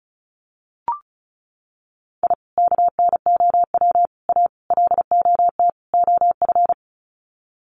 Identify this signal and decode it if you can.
Morse code, heard as a keyed tone, about 34 wpm, 700 Hz. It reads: I KNOW A LOT OF
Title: Sets of 5 Words Encore 35wpm